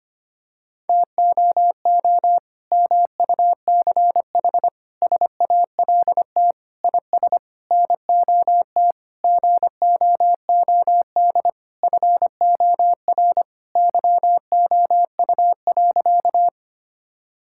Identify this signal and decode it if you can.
Morse code, heard as a keyed tone, about 25 wpm, 700 Hz. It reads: TOO MUCH SALT IS NOT GOOD FOR YOU.
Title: Sentences from Top 1000 Words - Review Part 5 - 25wpm